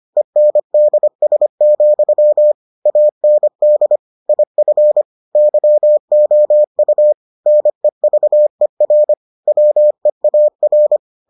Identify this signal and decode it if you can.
Morse code, heard as a keyed tone, about 25 wpm, 600 Hz. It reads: ENDS, AND IF YOU NEVER WEAR